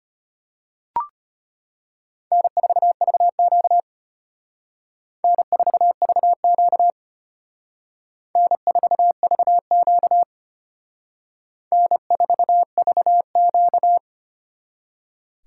Morse code, heard as a keyed tone, about 32 wpm, 700 Hz. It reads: N4VQ N4VQ N4VQ N4VQ